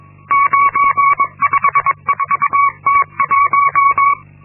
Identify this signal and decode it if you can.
Morse code, heard as a keyed tone, about 22 wpm, 1120 Hz. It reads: 954T1
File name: Linksmakalnis-2